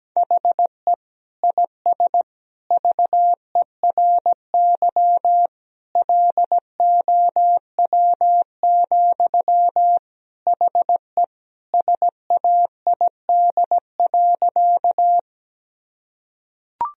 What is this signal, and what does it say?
Morse code, heard as a keyed tone, about 17 wpm, 700 Hz. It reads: HE IS VERY LOW, HE SAID.
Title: Sentences from Top 1000 Words - Review Part 1 - 17wpm